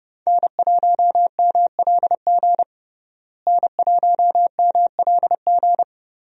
Morse code, heard as a keyed tone, about 32 wpm, 700 Hz. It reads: N1MLG N1MLG
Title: USA Call Signs - 5 Characters - Repeated 2x - 30wpm